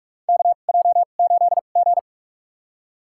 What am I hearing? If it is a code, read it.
Morse code, heard as a keyed tone, about 46 wpm, spaced as for 33 wpm, 700 Hz. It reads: KJ8G
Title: USA Call Signs - 2x Character Spacing - 45wpm